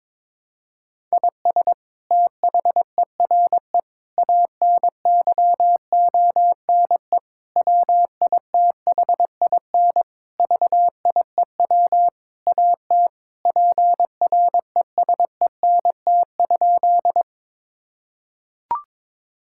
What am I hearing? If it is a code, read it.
Morse code, heard as a keyed tone, about 22 wpm, 700 Hz. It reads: IS THERE ANYONE WITHIN VIEW AT PRESENT?
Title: Sentences from Top 1000 Words - Review Part 6 - 22wpm